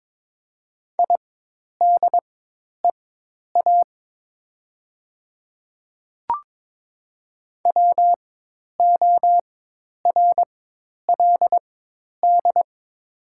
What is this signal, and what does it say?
Morse code, heard as a keyed tone, about 22 wpm, spaced as for 10 wpm, 700 Hz. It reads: IDEA WORLD
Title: 4x Character Spacing - Top 500 Words - 22wpm